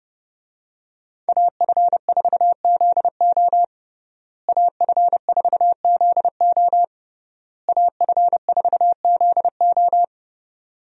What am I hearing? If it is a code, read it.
Morse code, heard as a keyed tone, about 30 wpm, 700 Hz. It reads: AF4ZO AF4ZO AF4ZO
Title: USA Call Signs - 5 Characters - Repeated 3x - 30wpm